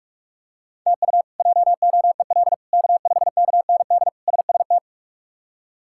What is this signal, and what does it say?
Morse code, heard as a keyed tone, about 42 wpm, 700 Hz. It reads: TU JOEL K5KND SST
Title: K1USN SST QSO Practice - 45wpm